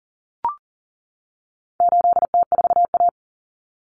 Morse code, heard as a keyed tone, about 40 wpm, 700 Hz. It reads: 8T4A
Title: Common Cut Numbers - 4-Digit 40wpm